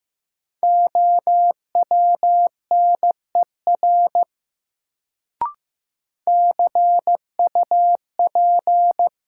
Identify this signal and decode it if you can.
Morse code, heard as a keyed tone, about 15 wpm, 700 Hz. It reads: OWNER CUP
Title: Sentences from Top 1600 Words - New Word Drill Alt - 15wpm